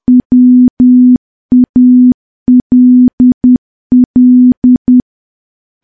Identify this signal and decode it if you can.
Morse code, heard as a keyed tone, about 10 wpm, 250 Hz. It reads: WALL
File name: morse / morse-training-10WPM